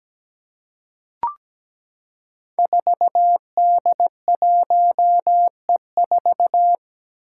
Morse code, heard as a keyed tone, about 17 wpm, 700 Hz. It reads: 4D1E4
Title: Five Challenging Letter-Number Combinations 17wpm